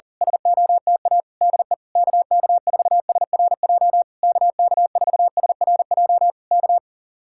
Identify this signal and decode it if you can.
Morse code, heard as a keyed tone, about 38 wpm, 700 Hz. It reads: SOTA DE KK4SRJ KK4SRJ K
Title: Speed-Racing Common QSO Phrases Encore - 40wpm